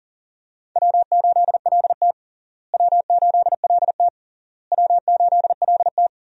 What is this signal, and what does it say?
Morse code, heard as a keyed tone, about 40 wpm, 700 Hz. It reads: W8LT W8LT W8LT
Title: USA Call Signs - 4 Characters - Repeated 3x - 40wpm